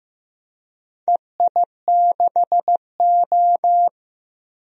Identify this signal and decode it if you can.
Morse code, heard as a keyed tone, about 15 wpm, 700 Hz. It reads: EI6O
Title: DXCC-100 Call Signs - 4 Characters - Repeated 2x - 15wpm